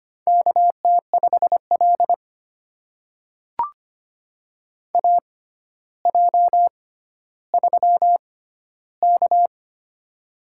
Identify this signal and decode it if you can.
Morse code, heard as a keyed tone, about 25 wpm, spaced as for 13 wpm, 700 Hz. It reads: KT5L AJ3K
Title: USA Call Signs - 4 Characters - 6x Character Spacing - 25wpm